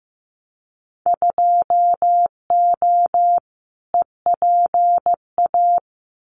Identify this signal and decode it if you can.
Morse code, heard as a keyed tone, about 15 wpm, 700 Hz. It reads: 2O EPA
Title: Winter Field Day Exchanges Encore 15wpm